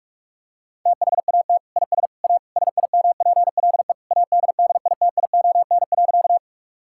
Morse code, heard as a keyed tone, about 46 wpm, 700 Hz. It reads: THAT IS A SIMPLE ADDITION.
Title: Sentences from Top 1000 Words - Review Part 6 - 45wpm